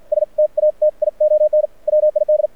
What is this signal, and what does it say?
Morse code, heard as a keyed tone, about 51 wpm, spaced as for 44 wpm, 610 Hz, with radio noise. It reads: STATION WID
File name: archived / CRSnet